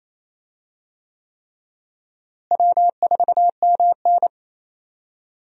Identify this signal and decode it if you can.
Morse code, heard as a keyed tone, about 28 wpm, 700 Hz. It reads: W4MN
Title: USA Call Signs - 4 Characters - Repeated 3x - 28wpm